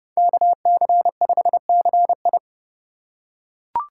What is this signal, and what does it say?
Morse code, heard as a keyed tone, about 28 wpm, 700 Hz. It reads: KC5CI